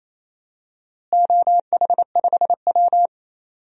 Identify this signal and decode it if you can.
Morse code, heard as a keyed tone, about 28 wpm, 700 Hz. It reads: OH5W
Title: DXCC-100 Call Signs - 4 Characters - Repeated 3x - 28wpm